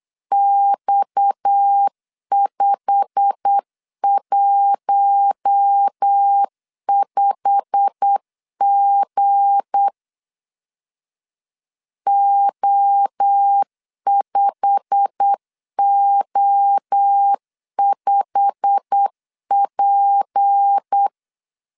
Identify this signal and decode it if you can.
Morse code, heard as a keyed tone, about 8 wpm, 800 Hz. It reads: X515G O5O5P
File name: LEZIONI / 8 YJ45 / 8 YJ45 8wpm 3214